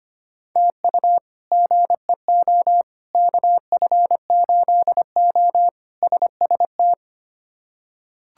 Morse code, heard as a keyed tone, about 25 wpm, 700 Hz. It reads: TU GEO KF8O SST
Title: K1USN SST QSO Practice - 25wpm